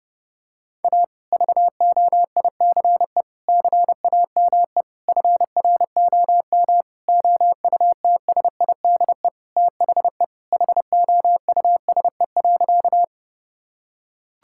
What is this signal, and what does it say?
Morse code, heard as a keyed tone, about 30 wpm, 700 Hz. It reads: A VOICE CAME FROM OUTSIDE THE HOUSE.
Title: Sentences from Top 1000 Words - Review Part 6 - 30wpm